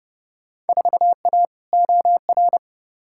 Morse code, heard as a keyed tone, about 30 wpm, 700 Hz. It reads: 4A OR
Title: Field Day Exchanges - 30wpm